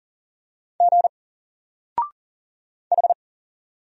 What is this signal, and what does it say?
Morse code, heard as a keyed tone, about 40 wpm, 700 Hz. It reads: G H
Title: Single Letter-Number Alt 40wpm